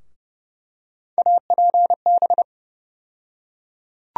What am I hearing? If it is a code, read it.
Morse code, heard as a keyed tone, about 32 wpm, 700 Hz. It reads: APB
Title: USA Call Sign Postfixes - 3 Letters 30wpm